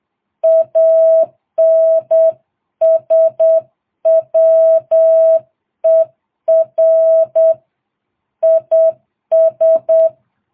Morse code, heard as a keyed tone, about 8 wpm, 650 Hz. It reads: ANSWER IS